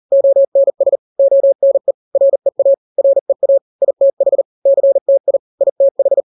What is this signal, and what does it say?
Morse code, heard as a keyed tone, about 40 wpm, 550 Hz. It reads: ONS ONE REA REA ITH CTI ITH
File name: n-grams / 3-grams 40 35 WPM